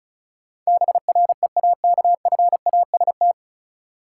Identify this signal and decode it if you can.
Morse code, heard as a keyed tone, about 36 wpm, 700 Hz. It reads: BREAKFAST
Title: Sentences from Top 1500 Words - New Word Drill - 35wpm